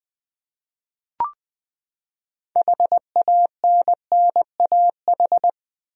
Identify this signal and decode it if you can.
Morse code, heard as a keyed tone, about 20 wpm, 700 Hz. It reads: HANNAH
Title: USA Common Names - 20wpm